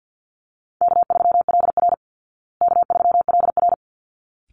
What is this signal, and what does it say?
Morse code, heard as a keyed tone, about 50 wpm, 700 Hz. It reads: K3LR K3LR
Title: Call Signs Repeated 3x - 50wpm